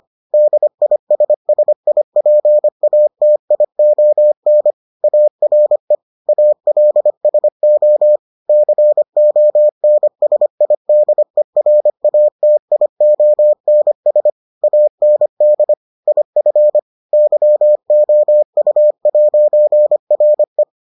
Morse code, heard as a keyed tone, about 25 wpm, 600 Hz. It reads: DISSIPATION ARE ALSO CONSIDERATIONS AND IF YOU'RE